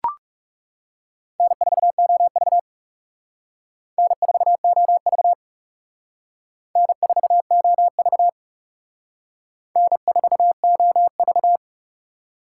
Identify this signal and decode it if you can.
Morse code, heard as a keyed tone, about 35 wpm, 700 Hz. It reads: N4OV N4OV N4OV N4OV
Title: Speed Racing - USA Call Signs 30wpm